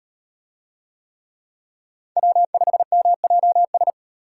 Morse code, heard as a keyed tone, about 34 wpm, 700 Hz. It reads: W5MJS